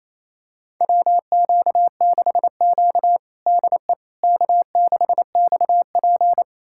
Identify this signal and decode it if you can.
Morse code, heard as a keyed tone, about 27 wpm, 700 Hz. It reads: WQ6Q DE K6XP
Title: USA Callsign DE Callsign - 4 Characters - 28wpm